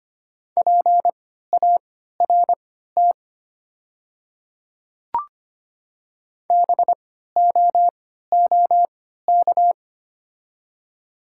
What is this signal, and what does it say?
Morse code, heard as a keyed tone, about 25 wpm, spaced as for 14 wpm, 700 Hz. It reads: PART BOOK
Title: 3x Character Spacing - Top 300 Words - 25wpm